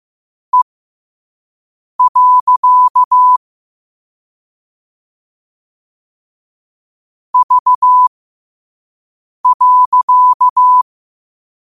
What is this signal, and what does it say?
Morse code, heard as a keyed tone, about 15 wpm, spaced as for 5 wpm, 1000 Hz. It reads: E. V.
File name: qotdfiles / 5wpm